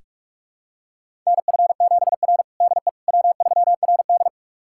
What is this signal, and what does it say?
Morse code, heard as a keyed tone, about 44 wpm, 690 Hz. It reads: NF7R DE W3RD
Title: USA Callsign DE Callsign - 4 Characters - 45wpm